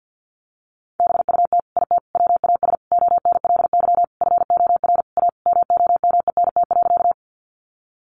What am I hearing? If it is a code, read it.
Morse code, heard as a keyed tone, about 50 wpm, 700 Hz. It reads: BUT IT WAS ONLY FOR A MOMENT.
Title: Sentences from Top 1000 Words 50wpm